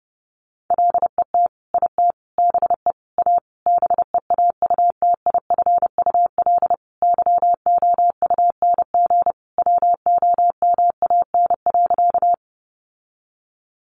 Morse code, heard as a keyed tone, about 30 wpm, 700 Hz. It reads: LET IT BE A BEAUTIFUL YOUNG WOMAN.
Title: Sentences from Top 1100 Words 30wpm